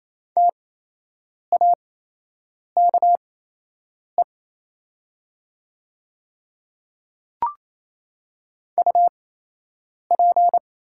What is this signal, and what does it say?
Morse code, heard as a keyed tone, about 27 wpm, spaced as for 8 wpm, 690 Hz. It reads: TAKE UP
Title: 8x Character Spacing - Top 200 Words - 28wpm